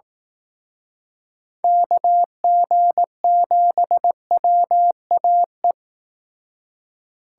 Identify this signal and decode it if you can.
Morse code, heard as a keyed tone, about 19 wpm, 700 Hz. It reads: KG7WAE
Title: Speed Racing - USA 2x3 Call Signs 15wpm